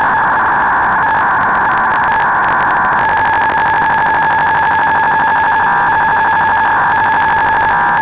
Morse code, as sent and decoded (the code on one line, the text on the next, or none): none